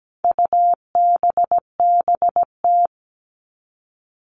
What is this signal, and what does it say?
Morse code, heard as a keyed tone, about 17 wpm, 700 Hz. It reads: UBBT
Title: All Cut Numbers - 4-Digit 17wpm